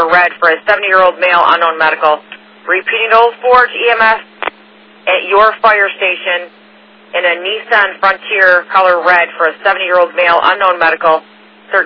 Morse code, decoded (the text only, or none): none